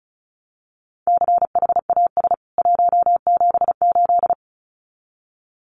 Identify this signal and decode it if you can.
Morse code, heard as a keyed tone, about 34 wpm, 700 Hz. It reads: CHAS 178
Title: CWT Exchange 35wpm